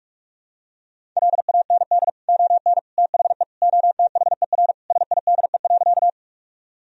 Morse code, heard as a keyed tone, about 45 wpm, 700 Hz. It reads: LAND ON THE OTHER SIDE.
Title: Sentences from Top 1000 Words - Review Part 1 - 45wpm